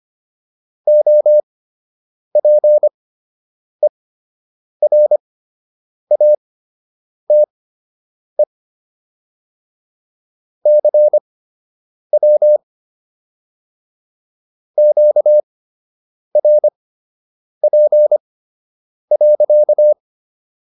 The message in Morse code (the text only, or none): OPERATE CW QRP.